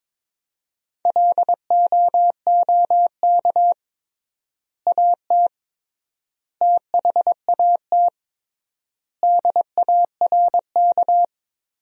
Morse code, heard as a keyed tone, about 22 wpm, 700 Hz. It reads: LOOK AT THAT DARK
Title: 3x Word-Spacing - Sentences from Top 500 Words - 22wpm